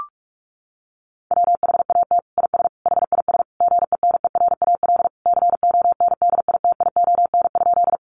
Morse code, heard as a keyed tone, about 44 wpm, 700 Hz. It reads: WHAT IS HIS GENERAL CONDITION?
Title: Sentences from Top 1000 Words 45wpm